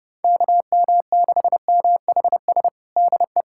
Morse code, heard as a keyed tone, about 29 wpm, 700 Hz. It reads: KM6MHS DE
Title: USA Callsign DE Callsign - 6 Characters - 30wpm